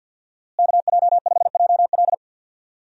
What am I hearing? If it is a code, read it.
Morse code, heard as a keyed tone, about 53 wpm, 700 Hz. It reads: KJ5JL
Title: USA Call Signs - 5 Characters - Repeated 3x - 50wpm